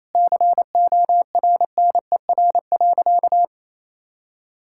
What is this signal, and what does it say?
Morse code, heard as a keyed tone, about 28 wpm, 700 Hz. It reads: CORNER.
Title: Sentences from Top 1600 Words 28wpm